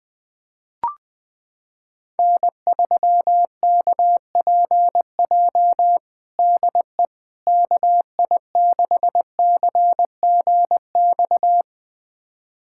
Morse code, heard as a keyed tone, about 20 wpm, 700 Hz. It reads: N3KPJ DE KI6CGX
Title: USA Callsign DE Callsign 20wpm